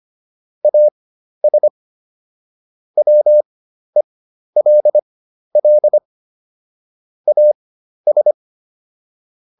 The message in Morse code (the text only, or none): AS WELL AS